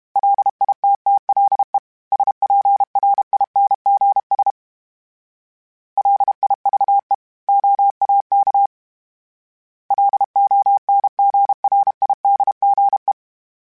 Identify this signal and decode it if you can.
Morse code, heard as a keyed tone, about 32 wpm, 800 Hz. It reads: LITTLE SPRINGS LIVE OAK LONGRIDGE
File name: fromSMs / FromSMs mp3s / Cities@30